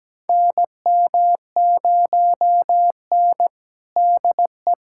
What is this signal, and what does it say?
Morse code, heard as a keyed tone, about 17 wpm, 700 Hz. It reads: NM0N DE
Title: USA Callsign DE Callsign - 4 Characters - 17wpm